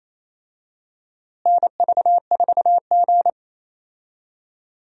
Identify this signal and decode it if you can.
Morse code, heard as a keyed tone, about 27 wpm, 700 Hz. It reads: NV4G
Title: USA Call Signs - 4 Characters - Repeated 3x - 28wpm